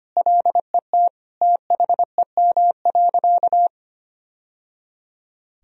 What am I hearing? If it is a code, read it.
Morse code, heard as a keyed tone, about 25 wpm, 700 Hz. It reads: LET THEM.
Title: Sentences from Top 1000 Words - Review Part 2 - 25wpm